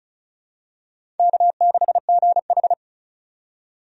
Morse code, heard as a keyed tone, about 34 wpm, 700 Hz. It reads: K6GH
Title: USA Call Signs - 4 Characters - 35wpm